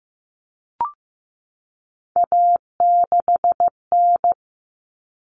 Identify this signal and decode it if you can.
Morse code, heard as a keyed tone, about 15 wpm, 700 Hz. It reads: A6N
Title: All Cut Numbers - 3-Digit 15wpm